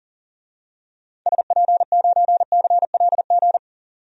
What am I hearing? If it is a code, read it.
Morse code, heard as a keyed tone, about 40 wpm, 700 Hz. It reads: SP9CLG
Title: DXCC-100 Call Signs - 6 Characters - 40wpm